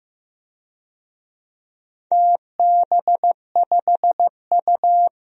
Morse code, heard as a keyed tone, about 15 wpm, 700 Hz. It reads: TB5U
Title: DXCC-100 Call Signs - 4 Characters - 15wpm